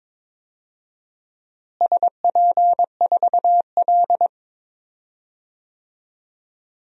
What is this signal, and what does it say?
Morse code, heard as a keyed tone, about 22 wpm, 700 Hz. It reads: SP4L
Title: DXCC-100 Call Signs - 4 Characters - Repeated 3x - 22wpm